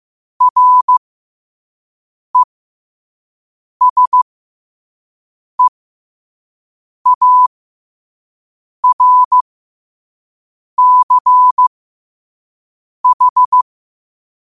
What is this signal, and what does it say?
Morse code, heard as a keyed tone, about 15 wpm, spaced as for 5 wpm, 1000 Hz. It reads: RESEARCH
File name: qotdfiles / 5wpm